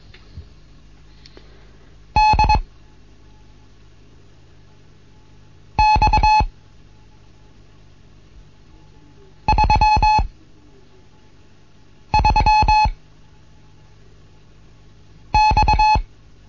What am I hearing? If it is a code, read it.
Morse code, heard as a keyed tone, about 21 wpm, spaced as for 3 wpm, 860 Hz. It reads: DX33X
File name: Morsen / Kurs A 15